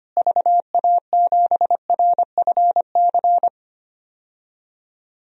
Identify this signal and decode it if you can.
Morse code, heard as a keyed tone, about 25 wpm, 700 Hz. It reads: VA7RFC
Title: DXCC-100 Call Signs - 6 Characters - Repeated 3x - 25wpm